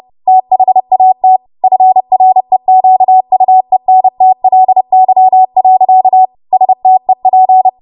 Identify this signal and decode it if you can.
Morse code, heard as a keyed tone, about 30 wpm, 750 Hz. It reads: THAT FREQUENTLY. STEP
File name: Morse / 30WPM